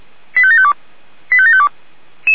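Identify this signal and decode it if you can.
Five-tone selective call (ZVEI), heard as quick runs of five tones, tones of about 70 ms. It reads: 85752 85752
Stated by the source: ZVEI five-tone selective call 85752, received via SDR on channel 464.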